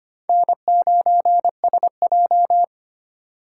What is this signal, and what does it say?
Morse code, heard as a keyed tone, about 25 wpm, 700 Hz. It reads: N9SJ
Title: USA Call Signs - 4 Characters - 4x Character Spacing - 25wpm